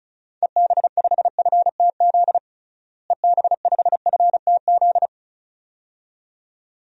Morse code, heard as a keyed tone, about 34 wpm, 700 Hz. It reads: EB5FTZ EB5FTZ
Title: DXCC-100 Call Signs - 6 Characters - Repeated 2x - 35wpm